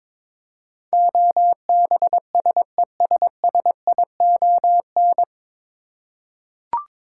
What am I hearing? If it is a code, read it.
Morse code, heard as a keyed tone, about 22 wpm, 700 Hz. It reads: OBSESSION
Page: Morse Code Ninja practice 22wpm